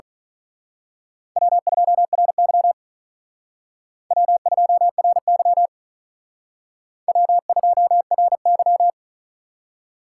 Morse code, heard as a keyed tone, about 43 wpm, 700 Hz. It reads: W2RY W2RY W2RY
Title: Speed Racing - USA 1x2 Call Signs 35wpm